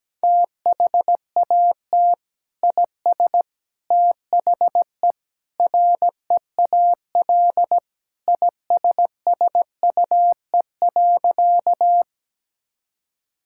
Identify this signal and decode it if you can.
Morse code, heard as a keyed tone, about 17 wpm, 700 Hz. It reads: THAT IS THE REAL ISSUE.
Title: Sentences from Top 1000 Words - Review Part 6 - 17wpm